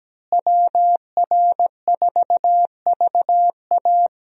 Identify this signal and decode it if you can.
Morse code, heard as a keyed tone, about 17 wpm, 700 Hz. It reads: WR4VA
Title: USA Call Signs - 5 Characters - Repeated 2x - 17wpm